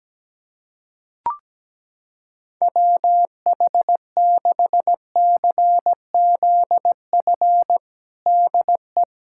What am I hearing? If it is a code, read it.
Morse code, heard as a keyed tone, about 17 wpm, 700 Hz. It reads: WH6CZF DE
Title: USA Callsign DE Callsign - 6 Characters - 17wpm